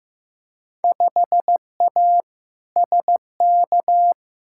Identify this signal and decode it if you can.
Morse code, heard as a keyed tone, about 15 wpm, 700 Hz. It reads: 5A SK
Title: Field Day Exchanges Encore - 15wpm